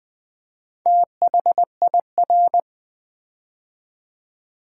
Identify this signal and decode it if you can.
Morse code, heard as a keyed tone, about 20 wpm, 700 Hz. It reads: THIR
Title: Build A Word - 2 Letters At A Time - 8-Letter Words - 20wpm